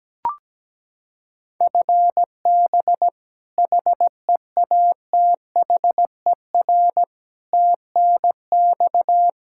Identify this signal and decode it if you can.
Morse code, heard as a keyed tone, about 17 wpm, 700 Hz. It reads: FB HEATHER TNX
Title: Practice QSO - Exchange 3 - More Info - 17wpm